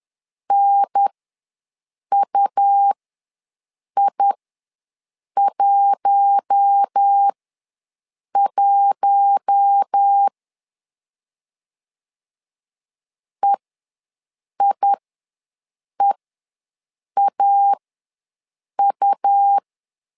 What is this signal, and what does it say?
Morse code, heard as a keyed tone, about 10 wpm, spaced as for 6 wpm, 800 Hz. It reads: NUI11 EIEAU
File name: LEZIONI / 4 Q1 ws 10wpm 2538